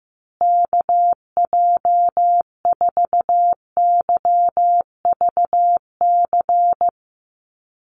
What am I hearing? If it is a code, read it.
Morse code, heard as a keyed tone, about 15 wpm, 700 Hz. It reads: KJ4YVC
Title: USA Call Signs Repeated 2x 15wpm